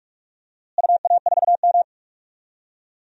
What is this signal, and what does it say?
Morse code, heard as a keyed tone, about 42 wpm, 700 Hz. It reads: UA4M
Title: DXCC-100 Call Signs - 4 Characters - Repeated 2x - 45wpm